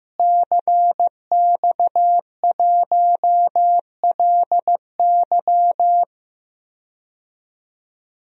Morse code, heard as a keyed tone, about 15 wpm, 700 Hz. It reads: CX1LY
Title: DXCC-100 Call Signs - 5 Characters - Repeated 2x - 15wpm